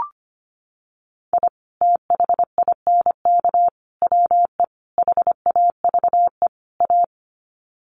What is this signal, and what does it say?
Morse code, heard as a keyed tone, about 25 wpm, 700 Hz. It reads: I THINK WE HAVE A
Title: Sets of 5 Words Encore 25wpm